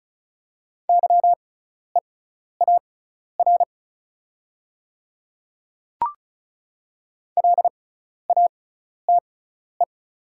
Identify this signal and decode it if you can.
Morse code, heard as a keyed tone, about 34 wpm, spaced as for 12 wpm, 690 Hz. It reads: YEAR LATE
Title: 6x Character Spacing - Top 500 Words - 35wpm